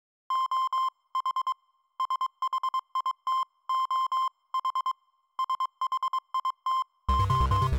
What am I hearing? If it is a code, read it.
Morse code, heard as a keyed tone, about 22 wpm, 1050 Hz. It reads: OH SHIT OH SHIT O
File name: NewJayTracks / Specific Games / Morse4